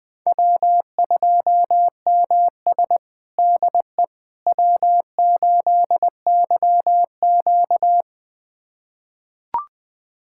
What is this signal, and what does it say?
Morse code, heard as a keyed tone, about 20 wpm, 700 Hz. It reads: W2MS DE W8YQ
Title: USA Callsign DE Callsign - 4 Characters - 20wpm